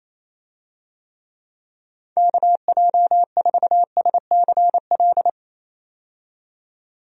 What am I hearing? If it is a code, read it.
Morse code, heard as a keyed tone, about 27 wpm, 700 Hz. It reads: KJ4SCL
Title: USA Call Signs - 6 Characters - 28wpm